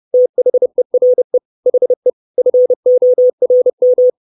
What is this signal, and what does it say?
Morse code, heard as a keyed tone, about 30 wpm, 500 Hz. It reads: THERE HE FORM
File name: 632 Words 30WPM